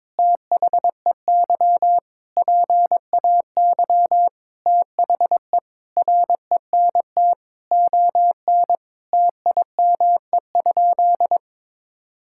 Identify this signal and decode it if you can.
Morse code, heard as a keyed tone, about 22 wpm, 700 Hz. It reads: THEY PAY THE RENT ON TIME?
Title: Sentences from Top 1000 Words - Review Part 5 - 22wpm